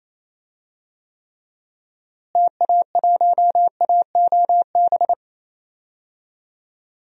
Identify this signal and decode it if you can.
Morse code, heard as a keyed tone, about 28 wpm, 700 Hz. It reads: TA1AOB
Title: DXCC-100 Call Signs - 6 Characters - Repeated 2x - 28wpm